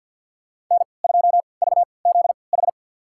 Morse code, heard as a keyed tone, about 50 wpm, spaced as for 28 wpm, 700 Hz. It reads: N2VZH
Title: USA Call Signs - 5 Characters - 3x Character Spacing - 50wpm